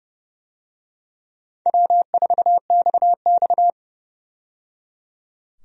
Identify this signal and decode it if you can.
Morse code, heard as a keyed tone, about 30 wpm, 700 Hz. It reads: W4XX
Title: USA Call Signs - 4 Characters - 30wpm